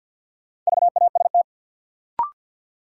Morse code, heard as a keyed tone, about 49 wpm, 700 Hz. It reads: VAST
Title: Sets of 4 Words - New Word Drill - 50wpm